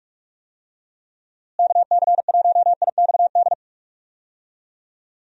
Morse code, heard as a keyed tone, about 46 wpm, 700 Hz. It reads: KC1IXD
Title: USA Call Signs - 6 Characters - Repeated 3x - 45wpm